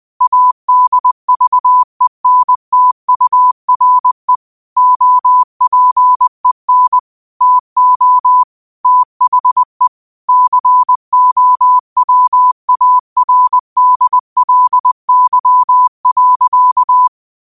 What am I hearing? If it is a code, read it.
Morse code, heard as a keyed tone, about 20 wpm, 1000 Hz. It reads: ADVENTURE OPEN TO THE COWARDLY.